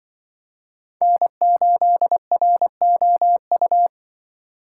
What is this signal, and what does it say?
Morse code, heard as a keyed tone, about 24 wpm, 700 Hz. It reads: N8ROU
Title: Speed Racing - USA 1x3 Call Signs 20wpm